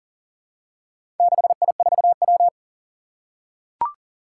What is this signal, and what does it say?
Morse code, heard as a keyed tone, about 42 wpm, 700 Hz. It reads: 6I4W
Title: Four Challenging Letter-Number Combinations 40wpm